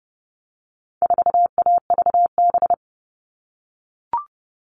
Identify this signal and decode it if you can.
Morse code, heard as a keyed tone, about 30 wpm, 700 Hz. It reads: 4AVB
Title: All Cut Numbers - 4-Digit 30wpm